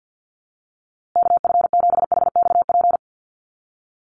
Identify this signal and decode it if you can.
Morse code, heard as a keyed tone, about 50 wpm, 700 Hz. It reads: KF7HXP
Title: USA Call Signs Repeated 3x 50wpm